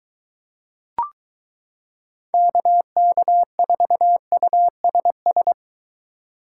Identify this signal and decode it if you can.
Morse code, heard as a keyed tone, about 23 wpm, 700 Hz. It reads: KK4USS